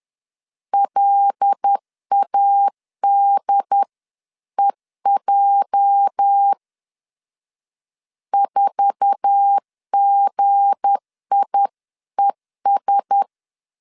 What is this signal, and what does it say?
Morse code, heard as a keyed tone, about 10 wpm, 800 Hz. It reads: LADEJ 4GIES
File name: LEZIONI / 8 YJ45 / 8 YJ4 10wpm 2633